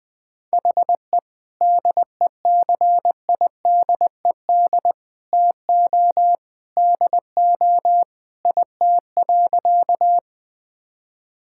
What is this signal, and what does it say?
Morse code, heard as a keyed tone, about 20 wpm, 700 Hz. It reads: HE DECIDED TO DO IT.
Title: Sentences with 3-Word Phrases - Spoken Beginning - 20wpm